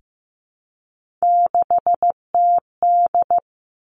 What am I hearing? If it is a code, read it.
Morse code, heard as a keyed tone, about 15 wpm, 700 Hz. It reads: 6TD
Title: Three Challenging Letter-Number Combinations 15wpm